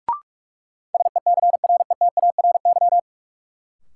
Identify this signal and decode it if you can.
Morse code, heard as a keyed tone, about 46 wpm, 700 Hz. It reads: SECRETARY
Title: Sentences from Top 1300 Words - New Word Drill - 45wpm